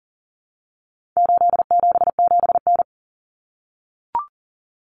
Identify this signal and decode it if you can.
Morse code, heard as a keyed tone, about 40 wpm, 700 Hz. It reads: 877N
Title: Common Cut Numbers - 4-Digit 40wpm